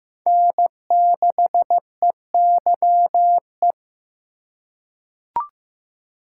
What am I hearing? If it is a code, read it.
Morse code, heard as a keyed tone, about 15 wpm, 700 Hz. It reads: N6EYE
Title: USA Call Signs - 5 Characters - 2x Character Spacing - 15wpm